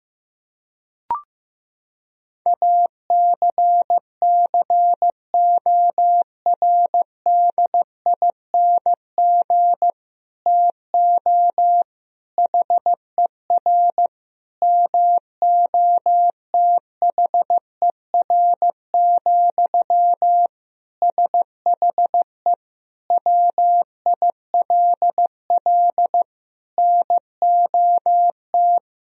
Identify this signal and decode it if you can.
Morse code, heard as a keyed tone, about 15 wpm, 700 Hz. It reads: ACCORDING TO HER MOTHER, SHE WILL NOT
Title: Sentences from Top 1000 Words - Review Part 4 - 15wpm